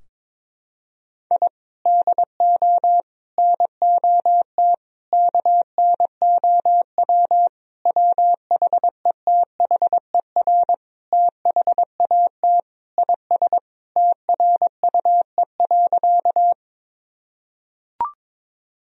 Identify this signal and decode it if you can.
Morse code, heard as a keyed tone, about 22 wpm, 700 Hz. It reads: I DO NOT KNOW WHETHER THAT IS TRUE.
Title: Sentences from Top 1000 Words - Review Part 3 - 22wpm